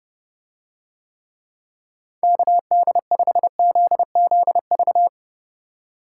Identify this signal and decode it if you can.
Morse code, heard as a keyed tone, about 28 wpm, 700 Hz. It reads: KD5ZZV